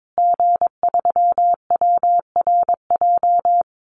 Morse code, heard as a keyed tone, about 22 wpm, 700 Hz. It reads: G3WRJ
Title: Call Signs Repeated 3x - 22wpm